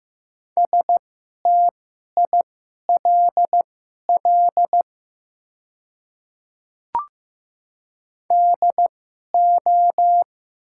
Morse code, heard as a keyed tone, about 15 wpm, spaced as for 11 wpm, 700 Hz. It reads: STILL DO
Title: 2x Character Spacing - Top 300 Words - 15wpm